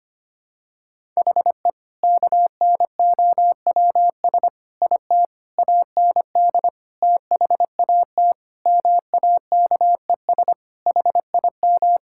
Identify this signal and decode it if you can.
Morse code, heard as a keyed tone, about 25 wpm, 700 Hz. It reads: HE KNOWS IT AND THAT MAKES HIM
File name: practice / 25wpm / Sentences with 2-Word Phrases - 25wpm